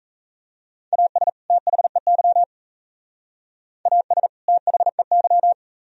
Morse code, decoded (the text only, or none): AS THEY AS THEY